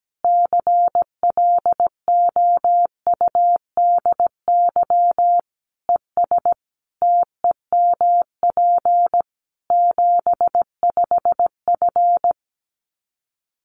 Morse code, heard as a keyed tone, about 17 wpm, 700 Hz. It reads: CLOUDY ES TEMP 75F
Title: Common QSO Phrases 17wpm